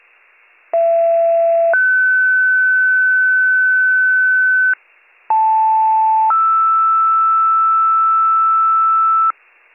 Two-tone page, heard as a one-second tone, then a three-second tone, about 670 then 1500 Hz, about 880 then 1320 Hz, with noise. It is heard from 0.7 to 9.3 s.